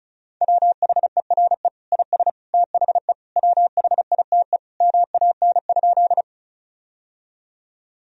Morse code, heard as a keyed tone, about 34 wpm, 700 Hz. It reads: WHERE IS THE WHITE MAN?
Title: Sentences from Top 1000 Words - Review Part 1 - 35wpm